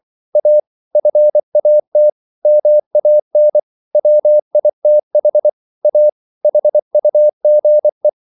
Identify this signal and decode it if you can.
Morse code, heard as a keyed tone, about 24 wpm, 600 Hz. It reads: A FAT MAN WITH A HUGE